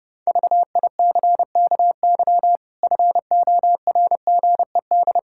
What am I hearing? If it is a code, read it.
Morse code, heard as a keyed tone, about 30 wpm, 700 Hz. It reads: VICKY FORGED
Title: Pangram Challenge v2 - 30wpm